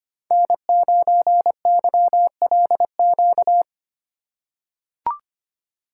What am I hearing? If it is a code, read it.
Morse code, heard as a keyed tone, about 25 wpm, 700 Hz. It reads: N9YLQ
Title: USA Call Signs - 3x Character Spacing - 25wpm